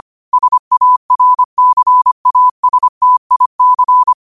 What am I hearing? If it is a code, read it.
Morse code, heard as a keyed tone, about 25 wpm, 1000 Hz. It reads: SARCASTIC